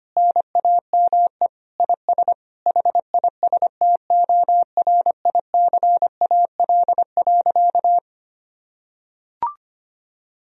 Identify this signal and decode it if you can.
Morse code, heard as a keyed tone, about 25 wpm, 700 Hz. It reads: NAME IS HISTORICAL.
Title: Sentences from Top 1000 Words - Review Part 4 - 25wpm